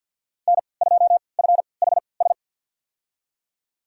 Morse code, heard as a keyed tone, about 50 wpm, spaced as for 28 wpm, 700 Hz. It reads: N2FHS
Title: USA Call Signs - 3x Character Spacing - 50wpm